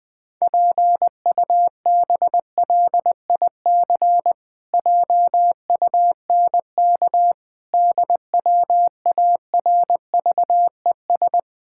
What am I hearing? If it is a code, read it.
Morse code, heard as a keyed tone, about 20 wpm, 700 Hz. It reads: PUBLIC JUNK DWARVES
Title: Pangram Challenge v2 - 20wpm